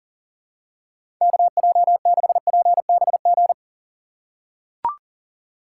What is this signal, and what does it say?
Morse code, heard as a keyed tone, about 40 wpm, 700 Hz. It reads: KJ6PBG
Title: USA Call Signs - 6 Characters - 2x Character Spacing - 40wpm